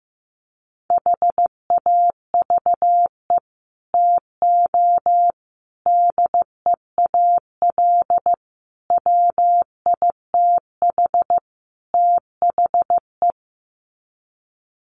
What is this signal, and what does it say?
Morse code, heard as a keyed tone, about 14 wpm, 700 Hz. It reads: HAVE TO DEAL WITH THE
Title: Sets of 5 Words Encore2 15wpm